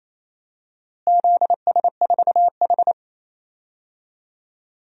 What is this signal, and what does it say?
Morse code, heard as a keyed tone, about 28 wpm, 700 Hz. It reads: ZS4H
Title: DXCC-100 Call Signs - 4 Characters - Repeated 3x - 28wpm